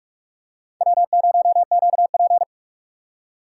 Morse code, heard as a keyed tone, about 44 wpm, 700 Hz. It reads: W0QP
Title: USA Call Signs - 4 Characters - 2x Character Spacing - 45wpm